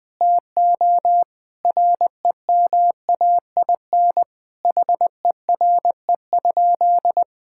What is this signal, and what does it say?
Morse code, heard as a keyed tone, about 20 wpm, 700 Hz. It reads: TO REMAIN HERE?
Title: Sentences from Top 1600 Words 20wpm